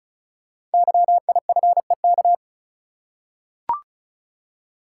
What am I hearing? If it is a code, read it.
Morse code, heard as a keyed tone, about 34 wpm, 690 Hz. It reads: YIFEK
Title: Code Groups - 5 Letters - 35wpm